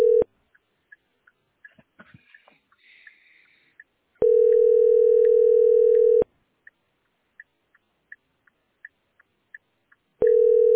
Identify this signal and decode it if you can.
Ringback tone (North America), 440 + 480 Hz.
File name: 9886cd74-ee19-4819-9dff